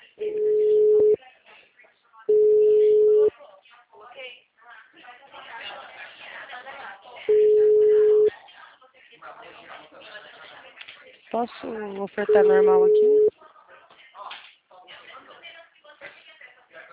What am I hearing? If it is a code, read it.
Ringback tone (Europe, 425 Hz), with speech.